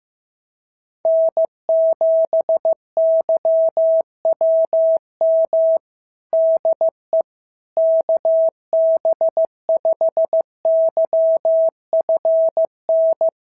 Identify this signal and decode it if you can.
Morse code, heard as a keyed tone, about 14 wpm, 650 Hz. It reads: N7YWM DE KB5YFN
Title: SOTA QSO Practice 15wpm